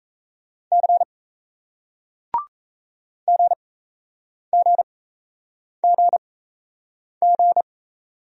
Morse code, heard as a keyed tone, about 38 wpm, spaced as for 8 wpm, 700 Hz. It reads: C GGGG